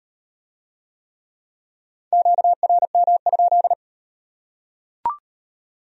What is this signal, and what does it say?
Morse code, heard as a keyed tone, about 33 wpm, 700 Hz. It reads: QRM?